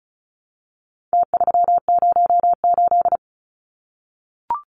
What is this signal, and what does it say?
Morse code, heard as a keyed tone, about 34 wpm, 700 Hz. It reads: T308
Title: Numbers - Contest Style - 4-Digit 35wpm